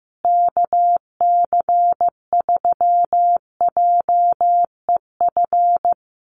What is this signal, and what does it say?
Morse code, heard as a keyed tone, about 15 wpm, 700 Hz. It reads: KC3JEF
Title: USA Call Signs Repeated 3x 15wpm